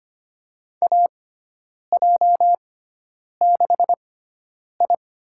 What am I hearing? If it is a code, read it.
Morse code, heard as a keyed tone, about 25 wpm, spaced as for 9 wpm, 700 Hz. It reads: AJ6I
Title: USA Call Signs - 6x Character Spacing - 25wpm